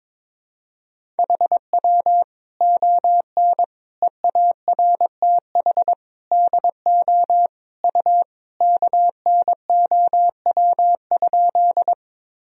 Morse code, heard as a keyed tone, about 22 wpm, 700 Hz. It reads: HW ON EARTH DO U KNOW?
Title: Sentences from Top 200 Words and Abbreviations 22wpm